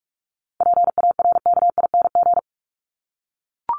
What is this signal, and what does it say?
Morse code, heard as a keyed tone, about 44 wpm, 700 Hz. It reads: PARKING
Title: Top 1501-1750 Words 45wpm